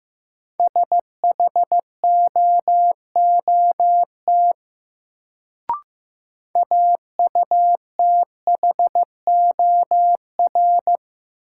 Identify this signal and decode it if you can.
Morse code, heard as a keyed tone, about 15 wpm, 700 Hz. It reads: SHOOT AUTHOR
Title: Sentences from Top 1200 Words - New Word Drill Alt - 15wpm